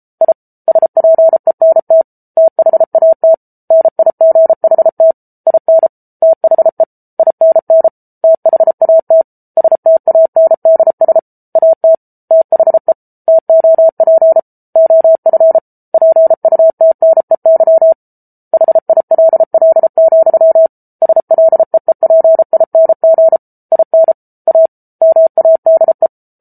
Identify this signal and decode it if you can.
Morse code, heard as a keyed tone, about 34 wpm, 650 Hz. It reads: I SPENT THAT NIGHT IN THE INN THAT STANDS AT THE TOP OF PUTNEY HILL, SLEEPING IN A MADE